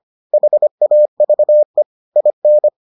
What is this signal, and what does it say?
Morse code, heard as a keyed tone, about 25 wpm, 600 Hz. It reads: HAVE IN